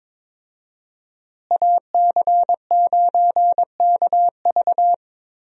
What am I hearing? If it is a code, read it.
Morse code, heard as a keyed tone, about 22 wpm, 700 Hz. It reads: AC9KV
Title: USA Call Signs - 5 Characters - Repeated 2x - 22wpm